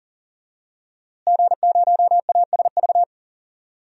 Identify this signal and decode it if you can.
Morse code, heard as a keyed tone, about 38 wpm, 700 Hz. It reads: G0ASV
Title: DXCC-100 Call Signs - 5 Characters - Repeated 3x - 40wpm